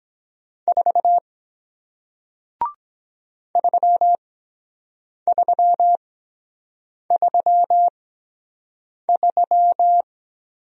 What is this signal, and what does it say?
Morse code, heard as a keyed tone, about 23 wpm, spaced as for 7 wpm, 700 Hz. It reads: 4 3333